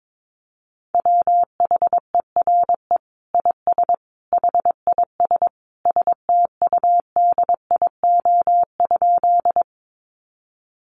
Morse code, heard as a keyed tone, about 22 wpm, 700 Hz. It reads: WHERE IS HIS STUDIO?
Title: Sentences from Top 1200 Words 22wpm